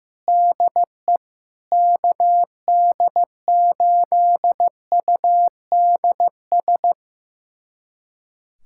Morse code, heard as a keyed tone, about 16 wpm, 700 Hz. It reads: DE KD8UDS
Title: USA Callsign DE Callsign - 6 Characters - 15wpm